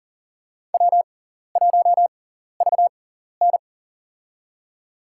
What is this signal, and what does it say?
Morse code, heard as a keyed tone, about 38 wpm, spaced as for 14 wpm, 700 Hz. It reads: W1VN
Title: USA Call Signs - 4 Characters - 6x Character Spacing - 40wpm